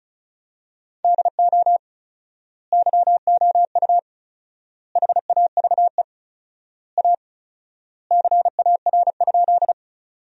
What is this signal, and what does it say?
Morse code, heard as a keyed tone, about 34 wpm, 700 Hz. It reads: DO YOU HAVE A CAR?
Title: 4x Word-Spacing - Sentences from Top 300 Words - 35wpm